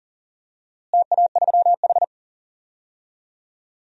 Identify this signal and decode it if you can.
Morse code, heard as a keyed tone, about 37 wpm, 700 Hz. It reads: TA3H
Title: DXCC-100 Call Signs - 4 Characters - Repeated 3x - 40wpm